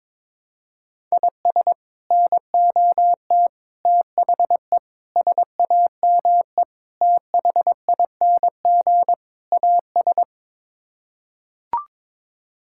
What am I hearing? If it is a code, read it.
Morse code, heard as a keyed tone, about 22 wpm, 700 Hz. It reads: IS NOT THE SAME THING AS
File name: practice / 22wpm / Sets of 6 Words - 22wpm